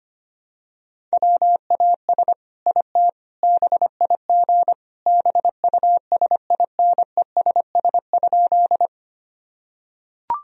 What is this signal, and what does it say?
Morse code, heard as a keyed tone, about 25 wpm, 700 Hz. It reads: WAS IT BIG BUSINESS?
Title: Sentences from Top 1000 Words - Review Part 1 - 25wpm